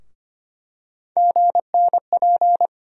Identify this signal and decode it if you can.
Morse code, heard as a keyed tone, about 25 wpm, 700 Hz. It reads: GNP
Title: USA Call Sign Postfixes - 3 Letters 25wpm